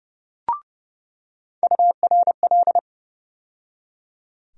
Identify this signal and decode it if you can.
Morse code, heard as a keyed tone, about 31 wpm, 700 Hz. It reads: URL
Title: USA Call Sign Postfixes - 3 Letters 30wpm